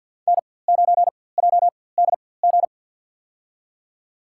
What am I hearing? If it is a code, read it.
Morse code, heard as a keyed tone, about 51 wpm, spaced as for 23 wpm, 700 Hz. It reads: N9JDG
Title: USA Call Signs - 5 Characters - 4x Character Spacing - 50wpm